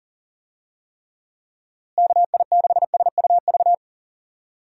Morse code, heard as a keyed tone, about 40 wpm, 700 Hz. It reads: KI6SUV
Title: USA Call Signs - 6 Characters - Repeated 2x - 40wpm